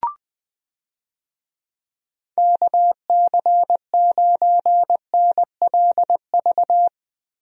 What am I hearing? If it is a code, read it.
Morse code, heard as a keyed tone, about 20 wpm, 700 Hz. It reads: KC9NLV
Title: USA Call Signs - 6 Characters - Repeated 3x - 20wpm